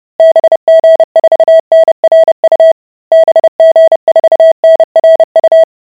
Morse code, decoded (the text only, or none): BG4NRU BG4NRU